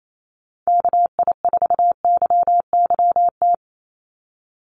Morse code, heard as a keyed tone, about 28 wpm, 700 Hz. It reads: KI4YYT
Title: USA Call Signs 28wpm